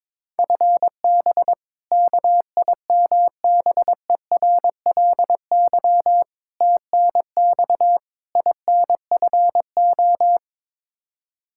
Morse code, heard as a keyed tone, about 22 wpm, 700 Hz. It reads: FB KIMBERLY TNX INFO
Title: Practice QSO - Exchange 3 - More Info - 22wpm